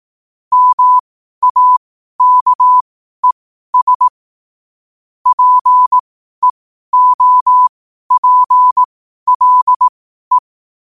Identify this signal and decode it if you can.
Morse code, heard as a keyed tone, about 18 wpm, spaced as for 12 wpm, 1000 Hz. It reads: MAKES PEOPLE
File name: qotdfiles / 12wpm